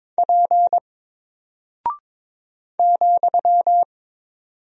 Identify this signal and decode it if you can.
Morse code, heard as a keyed tone, about 22 wpm, 700 Hz. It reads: P ,
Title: Single Character (Letter:Number:Punctuation:Prosign) 22wpm